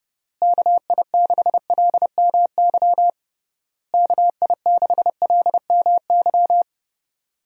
Morse code, heard as a keyed tone, about 32 wpm, 700 Hz. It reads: KI6LMY KI6LMY
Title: USA Call Signs - 6 Characters - Repeated 3x - 30wpm